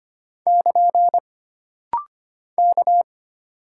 Morse code, heard as a keyed tone, about 25 wpm, 700 Hz. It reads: <KN> K
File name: Single Character (Letter:Number:Punctuation:Prosign) - Rapid-Fire - 25wpm